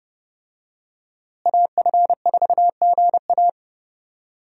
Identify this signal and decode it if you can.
Morse code, heard as a keyed tone, about 30 wpm, 700 Hz. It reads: AF4GA
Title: USA Call Signs - 5 Characters - Repeated 2x - 30wpm